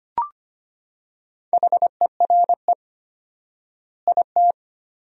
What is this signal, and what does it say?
Morse code, heard as a keyed tone, about 25 wpm, 700 Hz. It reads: HERE IT
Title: 4x Word-Spacing - Sentences from Top 200 Words - 25wpm